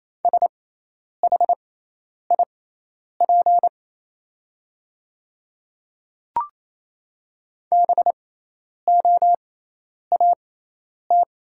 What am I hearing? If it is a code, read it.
Morse code, heard as a keyed tone, about 28 wpm, spaced as for 10 wpm, 700 Hz. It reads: SHIP BOAT
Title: 6x Character Spacing - Top 500 Words - 28wpm